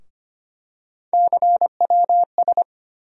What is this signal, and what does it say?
Morse code, heard as a keyed tone, about 25 wpm, 700 Hz. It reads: CWS
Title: USA Call Sign Postfixes - 3 Letters 25wpm